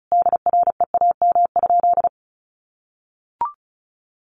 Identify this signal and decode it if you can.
Morse code, heard as a keyed tone, about 34 wpm, 700 Hz. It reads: DREAM?
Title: Sentences from Top 1500 Words 35wpm